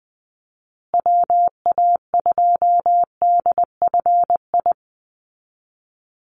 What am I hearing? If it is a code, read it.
Morse code, heard as a keyed tone, about 20 wpm, 700 Hz. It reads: WA2DFI
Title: Call Signs Repeated 2x - 20wpm